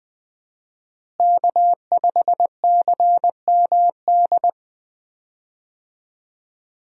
Morse code, heard as a keyed tone, about 20 wpm, 700 Hz. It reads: K5CMD
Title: USA Call Signs - 5 Characters - Repeated 2x - 20wpm